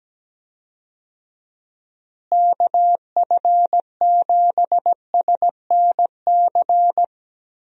Morse code, heard as a keyed tone, about 17 wpm, 700 Hz. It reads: KF7SNC
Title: USA Call Signs - 6 Characters - Repeated 2x - 17wpm